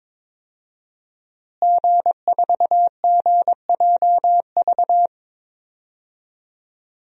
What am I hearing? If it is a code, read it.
Morse code, heard as a keyed tone, about 22 wpm, 700 Hz. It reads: G4GJV
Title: DXCC-100 Call Signs - 5 Characters - Repeated 3x - 22wpm